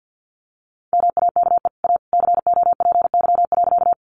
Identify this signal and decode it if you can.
Morse code, heard as a keyed tone, about 50 wpm, 700 Hz. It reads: MAKE A COPY.
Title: Sentences from Top 1400 Words 50wpm